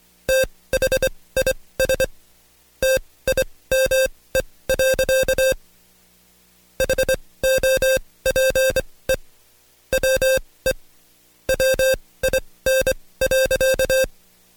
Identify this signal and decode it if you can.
Morse code, heard as a keyed tone, about 24 wpm, spaced as for 18 wpm, 530 Hz. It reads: THIS TIME. HOPE WE WIN.
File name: Code Quick / AudioDownld2 / Seg135-Stepper8 17WPM